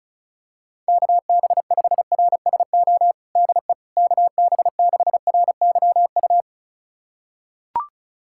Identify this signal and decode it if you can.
Morse code, heard as a keyed tone, about 36 wpm, 700 Hz. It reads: KB5RSO DE KB6RYU
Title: USA Callsign DE Callsign - 6 Characters - 35wpm